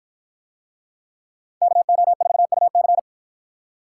Morse code, heard as a keyed tone, about 53 wpm, 700 Hz. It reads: KO4UC